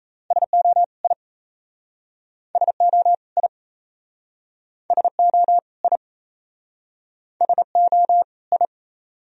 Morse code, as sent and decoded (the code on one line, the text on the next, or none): ... --- / .. / ... --- / .. / ... --- / .. / ... --- / ..
SO I SO I SO I SO I